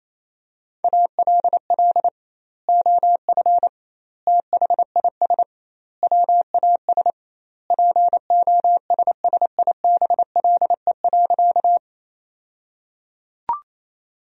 Morse code, heard as a keyed tone, about 28 wpm, 700 Hz. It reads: ALL OF THIS WAS POSSIBLE.
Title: 2x Word-Spacing - Sentences from Top 500 Words - 28wpm